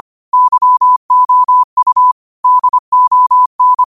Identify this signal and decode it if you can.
Morse code, heard as a keyed tone, about 25 wpm, 1000 Hz. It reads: YOU DON
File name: qotdfiles / 25wpm